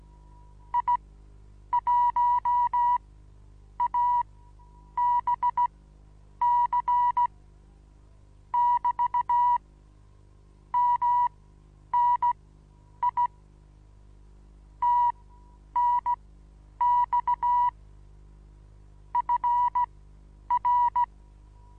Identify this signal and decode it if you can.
Morse code, heard as a keyed tone, about 17 wpm, spaced as for 9 wpm, 970 Hz. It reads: I1ABC <BT> MNI TNX FR
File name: Onlinelehrgang / morsen / DJ4UF Morsen25